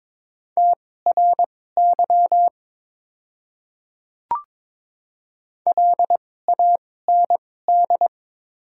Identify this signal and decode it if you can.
Morse code, heard as a keyed tone, about 22 wpm, spaced as for 16 wpm, 700 Hz. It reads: TRY LAND
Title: 2x Character Spacing - Top 300 Words - 22wpm